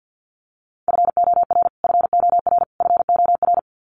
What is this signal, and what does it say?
Morse code, heard as a keyed tone, about 49 wpm, 700 Hz. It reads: FOR FOR FOR
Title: Top 500 Words Repeated 50wpm